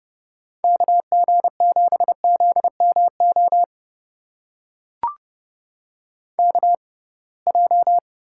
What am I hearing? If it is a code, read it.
Morse code, heard as a keyed tone, about 30 wpm, spaced as for 23 wpm, 700 Hz. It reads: KG7ZMO KJ
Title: USA Call Signs - 6 Characters - 6x Character Spacing - 30wpm